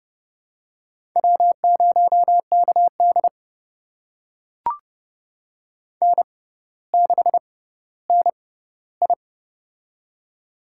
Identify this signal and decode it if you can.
Morse code, heard as a keyed tone, about 31 wpm, spaced as for 16 wpm, 700 Hz. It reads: W0KD N6NI
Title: USA Call Signs - 4 Characters - 6x Character Spacing - 30wpm